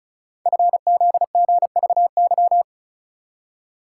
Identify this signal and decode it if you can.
Morse code, heard as a keyed tone, about 36 wpm, 700 Hz. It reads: FZGVY
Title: Code Groups - 5 Letters - 35wpm